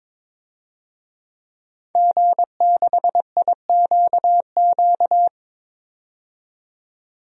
Morse code, heard as a keyed tone, about 22 wpm, 700 Hz. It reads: G6IQQ
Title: DXCC-100 Call Signs - 5 Characters - 22wpm